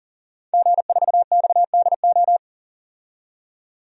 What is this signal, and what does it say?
Morse code, heard as a keyed tone, about 40 wpm, 700 Hz. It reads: G4XDO
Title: DXCC-100 Call Signs - 5 Characters - 40wpm